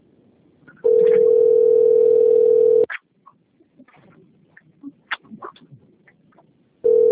Ringback tone (North America), 440 + 480 Hz.